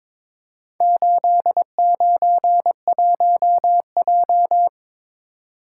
Morse code, heard as a keyed tone, about 22 wpm, 700 Hz. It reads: 891J
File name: practice / 22wpm / Four Letter-Number Combinations - 22wpm